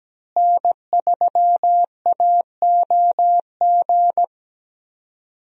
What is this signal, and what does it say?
Morse code, heard as a keyed tone, about 17 wpm, 700 Hz. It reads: N3AOG
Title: USA Call Signs - 5 Characters - Repeated 2x - 17wpm